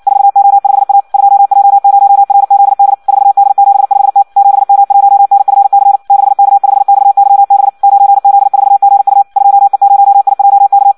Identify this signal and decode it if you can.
Morse code, heard as a keyed tone, about 56 wpm, 800 Hz. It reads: 4O5A 2J0RZD 4D75T 7M1NFC 6K5XYB 9Z4GU 2E0IJK
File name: Audio practica CW / qrq-tx-58 2